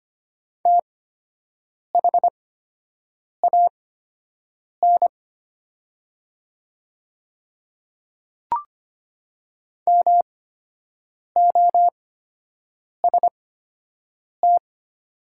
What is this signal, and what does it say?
Morse code, heard as a keyed tone, about 25 wpm, spaced as for 7 wpm, 700 Hz. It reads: THAN MOST
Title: 8x Character Spacing - Top 100 Words - 25wpm